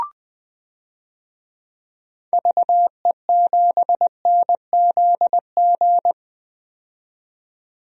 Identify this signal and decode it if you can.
Morse code, heard as a keyed tone, about 20 wpm, 700 Hz. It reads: VE7NZG
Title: DXCC-100 Call Signs - 6 Characters - 20wpm